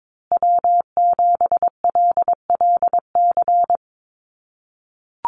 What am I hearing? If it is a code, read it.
Morse code, heard as a keyed tone, about 22 wpm, 700 Hz. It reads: W7LLC
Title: USA Call Signs Repeated 2x 22wpm